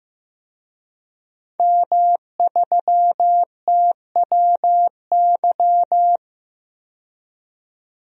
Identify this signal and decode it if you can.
Morse code, heard as a keyed tone, about 15 wpm, 700 Hz. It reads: M3TWY
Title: DXCC-100 Call Signs - 5 Characters - 15wpm